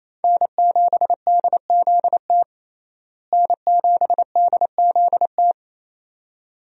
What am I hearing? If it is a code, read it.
Morse code, heard as a keyed tone, about 28 wpm, 700 Hz. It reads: N7DZT N7DZT
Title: USA Call Signs - 5 Characters - Repeated 2x - 28wpm